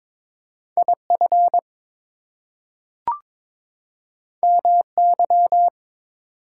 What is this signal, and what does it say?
Morse code, heard as a keyed tone, about 22 wpm, 700 Hz. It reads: IF MY
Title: Two Letter Words - 22wpm